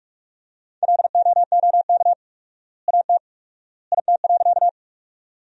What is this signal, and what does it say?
Morse code, heard as a keyed tone, about 41 wpm, 700 Hz. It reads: LOOK AT IT.
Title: 4x Word-Spacing - Sentences from Top 100 Words - 45wpm